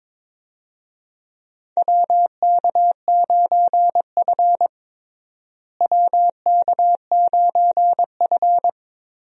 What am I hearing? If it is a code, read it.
Morse code, heard as a keyed tone, about 22 wpm, 700 Hz. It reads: WK9F WK9F
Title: USA Call Signs - 4 Characters - Repeated 3x - 22wpm